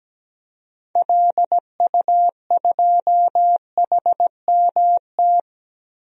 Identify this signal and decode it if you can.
Morse code, heard as a keyed tone, about 17 wpm, 700 Hz. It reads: LU2HMT
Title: DXCC-100 Call Signs - 6 Characters - 17wpm